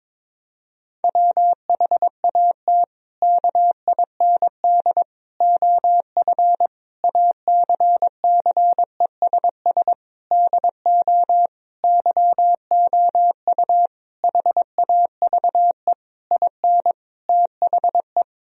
Morse code, heard as a keyed tone, about 21 wpm, 700 Hz. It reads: WHAT KIND OF ACCESS DO YOU HAVE IN THE
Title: Sentences from Top 1000 Words - Review Part 4 - 22wpm